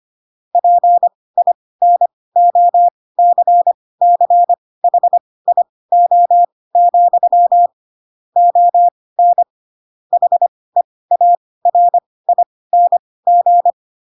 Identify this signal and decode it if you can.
Morse code, heard as a keyed tone, about 25 wpm, spaced as for 18 wpm, 700 Hz. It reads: PINOCCHIO, ON HEARING
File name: Pinocchio ch3 12 25 ik1hge